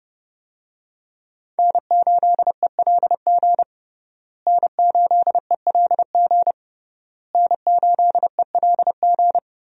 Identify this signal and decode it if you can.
Morse code, heard as a keyed tone, about 29 wpm, 700 Hz. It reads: N8ELG N8ELG N8ELG
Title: USA Call Signs - 5 Characters - Repeated 3x - 30wpm